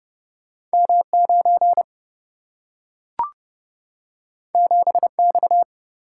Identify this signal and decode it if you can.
Morse code, heard as a keyed tone, about 29 wpm, 700 Hz. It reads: M9 7X
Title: Single-Letter Single-Number 30wpm